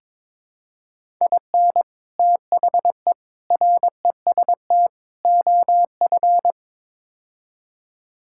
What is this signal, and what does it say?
Morse code, heard as a keyed tone, about 22 wpm, 700 Hz. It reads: IN THE REST OF
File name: practice / 22wpm / Build A Phrase - 5-Word Phrases - 22wpm